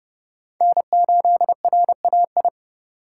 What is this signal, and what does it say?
Morse code, heard as a keyed tone, about 31 wpm, 700 Hz. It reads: N8RAI
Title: USA Call Signs - 5 Characters - 2x Character Spacing - 30wpm